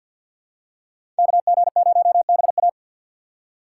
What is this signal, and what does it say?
Morse code, heard as a keyed tone, about 52 wpm, 690 Hz. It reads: KG0BA
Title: USA Call Signs - 2x Character Spacing - 50wpm